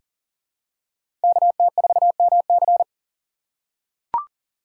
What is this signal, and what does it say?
Morse code, heard as a keyed tone, about 38 wpm, 700 Hz. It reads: KT4MC
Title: USA Call Signs - 5 Characters - 6x Character Spacing - 40wpm